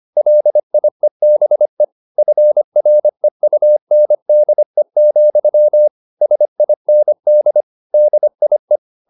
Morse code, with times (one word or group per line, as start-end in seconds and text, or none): 0.2-1.8 LIEBE
2.2-5.9 FREUNDE,
6.2-7.6 SIND
7.9-8.8 DIE